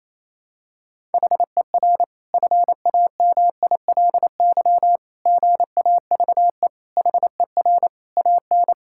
Morse code, heard as a keyed tone, about 28 wpm, 700 Hz. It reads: HER FAMILY GAVE HER AN
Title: Sentences from Top 1000 Words - Review Part 5 - 28wpm